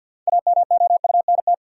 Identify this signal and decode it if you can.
Morse code, heard as a keyed tone, about 51 wpm, 700 Hz. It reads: AMOUNT
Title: Sets of 3 Words - New Word Drill - 50wpm